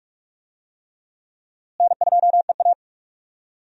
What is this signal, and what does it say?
Morse code, heard as a keyed tone, about 48 wpm, 700 Hz. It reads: N2EA
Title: USA Call Signs - 4 Characters - Repeated 3x - 45wpm